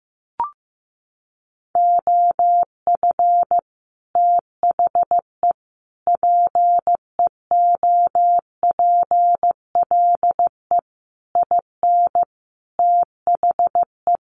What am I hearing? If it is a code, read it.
Morse code, heard as a keyed tone, about 15 wpm, 700 Hz. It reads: OF THE PEOPLE IN THE
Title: Sets of 5 Words Encore 15wpm